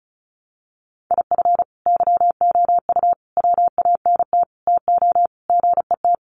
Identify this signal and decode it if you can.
Morse code, heard as a keyed tone, about 34 wpm, 700 Hz. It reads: IF YOU WANT TO GET
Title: Sets of 5 Words Encore 35wpm